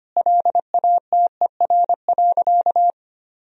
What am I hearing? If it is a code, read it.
Morse code, heard as a keyed tone, about 25 wpm, 700 Hz. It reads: LATER.
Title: Sentences from Top 1000 Words - Review Part 3 - 25wpm